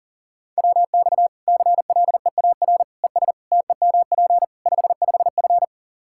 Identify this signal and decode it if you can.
Morse code, heard as a keyed tone, about 40 wpm, 700 Hz. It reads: WX CLEAR ES TEMP 55F
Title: Practice QSO - Exchange 2 - Info - 40wpm